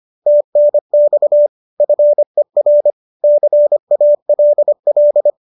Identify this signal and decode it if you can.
Morse code, heard as a keyed tone, about 25 wpm, 590 Hz. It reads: TNX FER CALL